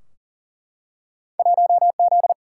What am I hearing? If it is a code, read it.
Morse code, heard as a keyed tone, about 40 wpm, 700 Hz. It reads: 1Z
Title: Single-Letter Single-Number 40wpm